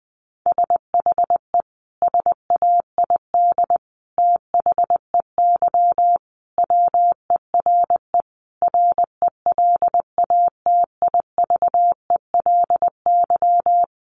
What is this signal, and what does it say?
Morse code, heard as a keyed tone, about 20 wpm, 700 Hz. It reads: SHE SAID THEY WERE RELATIVELY
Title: Sentences from Top 1100 Words 20wpm